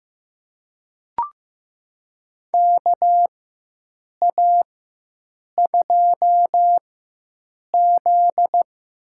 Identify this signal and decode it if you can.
Morse code, heard as a keyed tone, about 16 wpm, spaced as for 7 wpm, 700 Hz. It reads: KA2Z
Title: USA Call Signs - 6 Characters - 4x Character Spacing - 15wpm